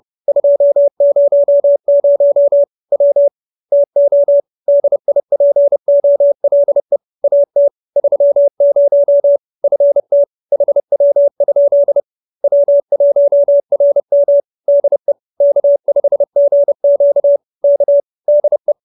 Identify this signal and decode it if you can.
Morse code, heard as a keyed tone, about 30 wpm, 570 Hz. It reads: 200 W TO DIPOLE AT 30 FT HW? W1RM DE K5GQ K DE
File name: QSO308 30